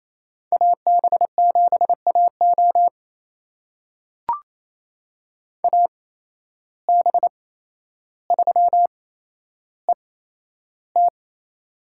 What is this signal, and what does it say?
Morse code, heard as a keyed tone, about 28 wpm, spaced as for 12 wpm, 700 Hz. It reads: AB7AO AB3ET
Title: USA Call Signs - 5 Characters - 8x Character Spacing - 28wpm